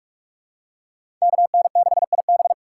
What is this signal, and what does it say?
Morse code, heard as a keyed tone, about 41 wpm, 700 Hz. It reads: KN6IB
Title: USA Call Signs - 5 Characters - 4x Character Spacing - 45wpm